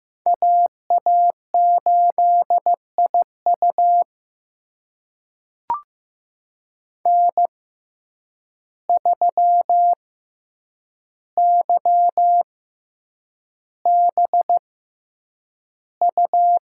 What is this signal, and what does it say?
Morse code, heard as a keyed tone, about 15 wpm, spaced as for 8 wpm, 700 Hz. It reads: AA8IU N3YBU
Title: USA Call Signs - 5 Characters - 6x Character Spacing - 15wpm